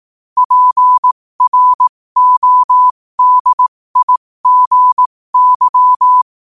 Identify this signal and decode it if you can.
Morse code, heard as a keyed tone, about 18 wpm, spaced as for 15 wpm, 1000 Hz. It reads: PRODIGY
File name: qotdfiles / 15wpm